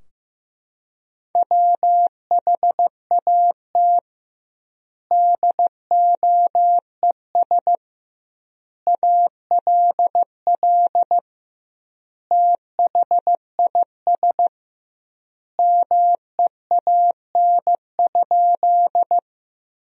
Morse code, heard as a keyed tone, about 16 wpm, 700 Hz. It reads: WHAT DOES ALL THIS MEAN?
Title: 2x Word-Spacing - Sentences from Top 200 Words - 15wpm